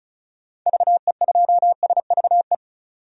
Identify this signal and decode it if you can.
Morse code, heard as a keyed tone, about 36 wpm, 700 Hz. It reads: VE2SVE
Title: DXCC-100 Call Signs - 6 Characters - 35wpm